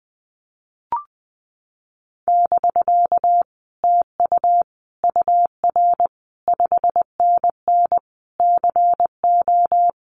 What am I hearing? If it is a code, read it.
Morse code, heard as a keyed tone, about 19 wpm, 700 Hz. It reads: <BK> TU UR 5NN CO
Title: Common QSO Phrases Encore 20wpm